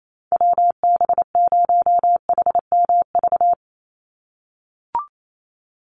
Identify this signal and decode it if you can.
Morse code, heard as a keyed tone, about 28 wpm, 700 Hz. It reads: WB0HMV
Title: USA Call Signs 28wpm